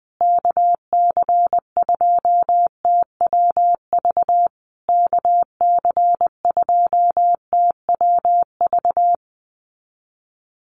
Morse code, heard as a keyed tone, about 20 wpm, 700 Hz. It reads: KC2TWV KC2TWV
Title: USA Call Signs Repeated 3x 20wpm